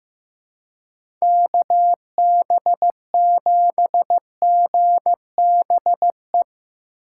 Morse code, heard as a keyed tone, about 15 wpm, 700 Hz. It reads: KB7GBE
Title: USA Call Signs - 2x Character Spacing - 15wpm